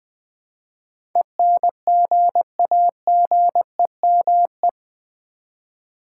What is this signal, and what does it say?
Morse code, heard as a keyed tone, about 20 wpm, 700 Hz. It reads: ENGAGEME
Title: Build A Word - 4 Letters At A Time - 10-Letter Words - 20wpm